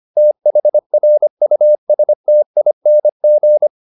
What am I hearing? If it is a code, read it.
Morse code, heard as a keyed tone, about 25 wpm, 600 Hz. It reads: THRUSTING